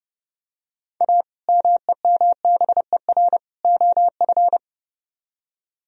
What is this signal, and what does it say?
Morse code, heard as a keyed tone, about 29 wpm, 700 Hz. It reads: A MEMBER OF